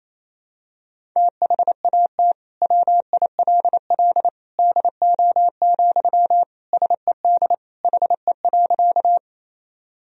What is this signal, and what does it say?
Morse code, heard as a keyed tone, about 28 wpm, 700 Hz. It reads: THAT WILL DO, SED HE.
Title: Sentences from Top 100 Words and Abbreviations 28wpm